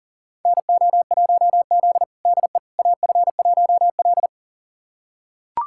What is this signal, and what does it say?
Morse code, heard as a keyed tone, about 40 wpm, 700 Hz. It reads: NO1Z DE AF1L
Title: USA Callsign DE Callsign - 4 Characters - 40wpm